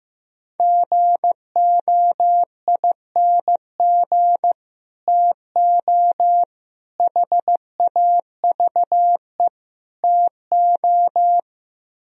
Morse code, heard as a keyed tone, about 15 wpm, 700 Hz. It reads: GOING TO HAVE TO
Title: Speed-Racing - Sets of 4 Words 15wpm